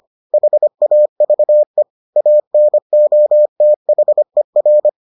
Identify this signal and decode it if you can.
Morse code, heard as a keyed tone, about 25 wpm, 600 Hz. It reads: HAVE ANOTHER